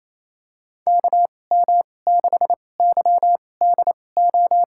Morse code, heard as a keyed tone, about 28 wpm, spaced as for 20 wpm, 700 Hz. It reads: KM6YDO
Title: USA Call Signs - 6 Characters - 2x Character Spacing - 28wpm